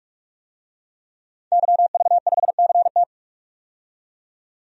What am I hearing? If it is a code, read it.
Morse code, heard as a keyed tone, about 46 wpm, 700 Hz. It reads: YV5CT
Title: DXCC-100 Call Signs - 5 Characters - 45wpm